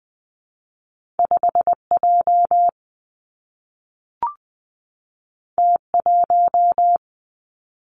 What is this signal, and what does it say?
Morse code, heard as a keyed tone, about 20 wpm, 700 Hz. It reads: 5 J T 1
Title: Two Challenging Letter-Number Combinations 20wpm